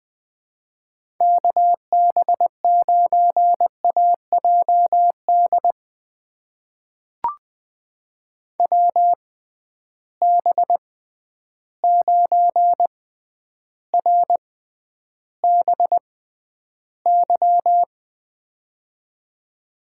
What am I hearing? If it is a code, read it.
Morse code, heard as a keyed tone, about 21 wpm, spaced as for 10 wpm, 700 Hz. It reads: KB9AJD WB9RBY
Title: USA Call Signs - 6 Characters - 6x Character Spacing - 20wpm